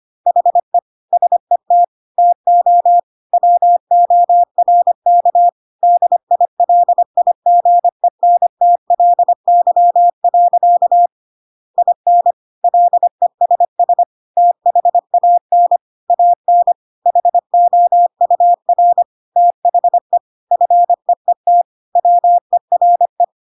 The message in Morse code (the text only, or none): HE SET TO WORK DILIGENTLY. IN LESS THAN AN HOUR THE FEET WERE